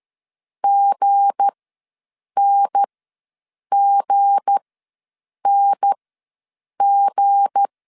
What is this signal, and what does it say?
Morse code, heard as a keyed tone, about 13 wpm, spaced as for 7 wpm, 800 Hz. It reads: GNGNG
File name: LEZIONI / 7 GF9Z / 7 G ws 12wpm 3140